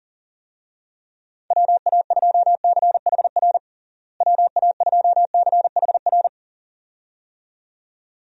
Morse code, heard as a keyed tone, about 40 wpm, 700 Hz. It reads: WA2CHR WA2CHR
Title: USA Call Signs - 6 Characters - Repeated 2x - 40wpm